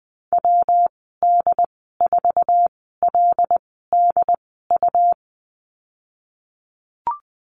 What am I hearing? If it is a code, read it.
Morse code, heard as a keyed tone, about 20 wpm, spaced as for 15 wpm, 700 Hz. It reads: WD4LDU
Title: 128 Lesson 21g - Number 6 - Call Signs 20wpm